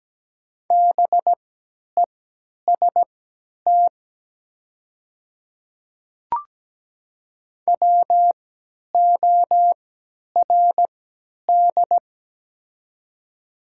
Morse code, heard as a keyed tone, about 17 wpm, spaced as for 10 wpm, 700 Hz. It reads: BEST WORD
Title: 3x Character Spacing - Top 500 Words - 17wpm